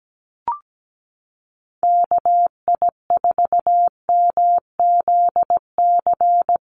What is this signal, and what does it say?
Morse code, heard as a keyed tone, about 17 wpm, 700 Hz. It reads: KI4MZC
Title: Call Signs Repeated 2x - 17wpm